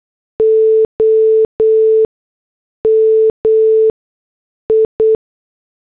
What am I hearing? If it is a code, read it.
Morse code, heard as a keyed tone, about 8 wpm, spaced as for 6 wpm, 440 Hz. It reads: OMI